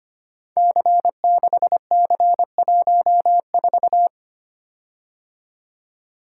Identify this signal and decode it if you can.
Morse code, heard as a keyed tone, about 25 wpm, 700 Hz. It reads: C6C14
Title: Code Groups - 5 Letter:Numbers - 25wpm